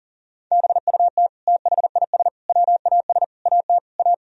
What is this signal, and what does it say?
Morse code, heard as a keyed tone, about 40 wpm, 700 Hz. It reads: BUT THIS WAS AT A
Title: Sentences from Top 1000 Words - Review Part 4 - 40wpm